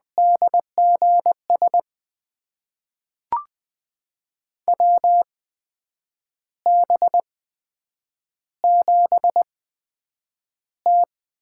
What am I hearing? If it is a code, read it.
Morse code, heard as a keyed tone, about 20 wpm, spaced as for 8 wpm, 700 Hz. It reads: DGS WB7T
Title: USA Call Signs - 6 Characters - 8x Character Spacing - 20wpm